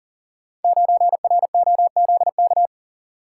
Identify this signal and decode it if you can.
Morse code, heard as a keyed tone, about 43 wpm, 690 Hz. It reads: 9ROZK
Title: Code Groups - 5 Letter:Numbers - 40wpm